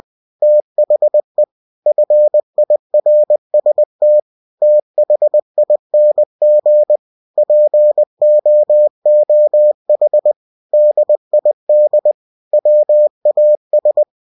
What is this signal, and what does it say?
Morse code, heard as a keyed tone, about 20 wpm, 600 Hz. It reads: THE FIRST THING POOH DID WAS
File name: cw-books / winnie-the-pooh / winnie-20wpm-page-0049